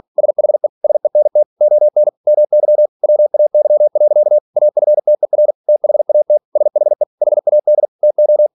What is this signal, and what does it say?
Morse code, heard as a keyed tone, about 46 wpm, 600 Hz. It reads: SHE SENT ON MY WAY. AFTER THAT SHE HAD TO
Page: Never 47wpm